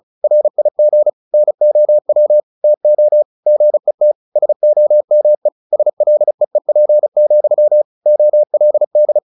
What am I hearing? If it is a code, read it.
Morse code, heard as a keyed tone, about 34 wpm, 600 Hz. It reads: RIG NOW TO GET SOME SLEEP, OLD
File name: the old man and the sea 2 35wpm